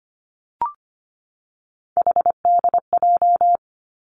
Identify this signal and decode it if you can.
Morse code, heard as a keyed tone, about 25 wpm, 700 Hz. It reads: HDJ
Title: Three Challenging Letter-Number Combinations 25wpm